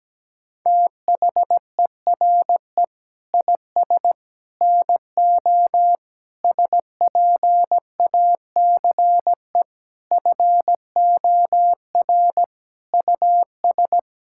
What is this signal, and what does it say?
Morse code, heard as a keyed tone, about 17 wpm, 700 Hz. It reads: THERE IS NO SPACE FOR US
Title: Sentences from Top 1000 Words - Review Part 5 - 17wpm